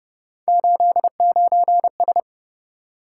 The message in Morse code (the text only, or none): 89S